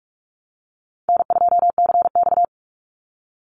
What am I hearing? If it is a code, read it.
Morse code, heard as a keyed tone, about 44 wpm, 700 Hz. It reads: N2CX
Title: Call Signs - 45wpm